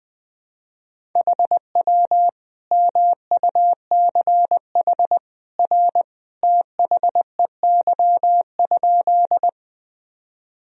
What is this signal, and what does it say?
Morse code, heard as a keyed tone, about 20 wpm, 700 Hz. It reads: HW MUCH R THEY?
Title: Sentences from Top 200 Words and Abbreviations 20wpm